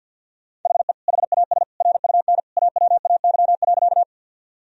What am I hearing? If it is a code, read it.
Morse code, heard as a keyed tone, about 50 wpm, 700 Hz. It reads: HE HAS RUN AWAY.
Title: Sentences from Top 1000 Words - Review Part 1 - 50wpm